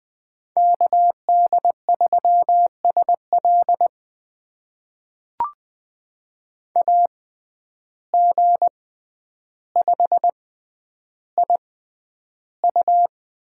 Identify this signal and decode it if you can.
Morse code, heard as a keyed tone, about 20 wpm, spaced as for 10 wpm, 700 Hz. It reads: KD3SL AG5IU
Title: USA Call Signs - 5 Characters - 6x Character Spacing - 20wpm